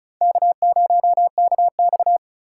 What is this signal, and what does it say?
Morse code, heard as a keyed tone, about 36 wpm, 700 Hz. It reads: K0KX
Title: K1USN SST QSO Practice - 35wpm